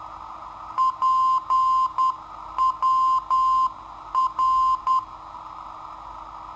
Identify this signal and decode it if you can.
Morse code, heard as a keyed tone, about 10 wpm, 1040 Hz. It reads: PWR
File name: sitoariold / beacon hk